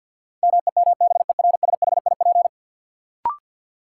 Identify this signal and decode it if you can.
Morse code, heard as a keyed tone, about 53 wpm, 700 Hz. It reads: MEMBERSHIP
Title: Sentences from Top 1400 Words - New Word Drill - 50wpm